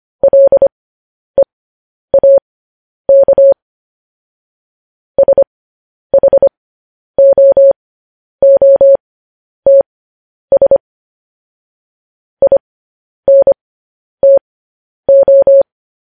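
Morse code, heard as a keyed tone, about 25 wpm, spaced as for 10 wpm, 550 Hz. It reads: LEAK SHOOTS INTO
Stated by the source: Morse code news headlines sent at 10 words per minute.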